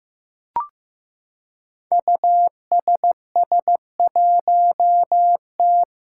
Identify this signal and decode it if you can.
Morse code, heard as a keyed tone, about 15 wpm, 700 Hz. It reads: USS1T
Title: Five Challenging Letter-Number Combinations 15wpm